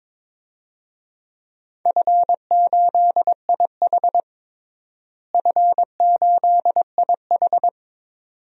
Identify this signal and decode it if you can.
Morse code, heard as a keyed tone, about 22 wpm, 700 Hz. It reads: F8IH F8IH
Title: DXCC-100 Call Signs - 4 Characters - Repeated 2x - 22wpm